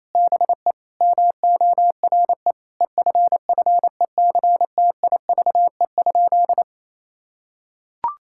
Morse code, heard as a keyed tone, about 28 wpm, 700 Hz. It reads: BE MORE EFFECTIVE?
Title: Sentences from Top 1000 Words - Review Part 6 - 28wpm